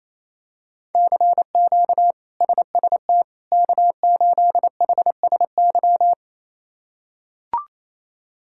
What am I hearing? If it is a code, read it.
Morse code, heard as a keyed tone, about 28 wpm, 700 Hz. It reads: CQ SST K8HSY
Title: K1USN SST QSO Practice - 28wpm